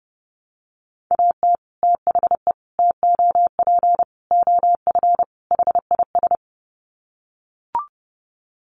Morse code, heard as a keyed tone, about 28 wpm, 700 Hz. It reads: AT THE TOP OF HIS
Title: Sets of 5 Words Encore 30wpm